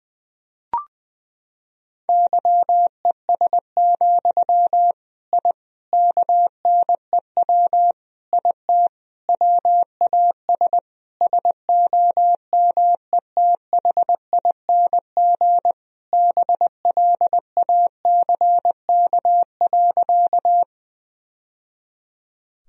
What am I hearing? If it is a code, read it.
Morse code, heard as a keyed tone, about 20 wpm, 700 Hz. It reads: YES, I KNEW IT WAS SOMETHING BLACK.
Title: Sentences from Top 1000 Words - Review Part 2 - 20wpm